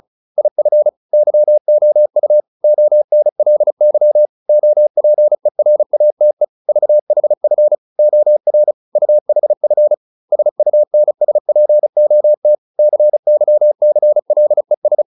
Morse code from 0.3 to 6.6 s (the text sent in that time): IF YOU ONLY OPERATE